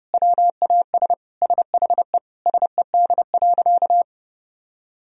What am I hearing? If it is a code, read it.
Morse code, heard as a keyed tone, about 30 wpm, 700 Hz. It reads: WAS SHE SED.
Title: Sentences from Top 400 Words and Abbreviations 30wpm